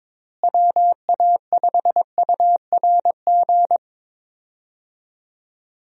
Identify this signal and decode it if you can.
Morse code, heard as a keyed tone, about 22 wpm, 700 Hz. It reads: WA5URG
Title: USA Call Signs - 6 Characters - Repeated 2x - 22wpm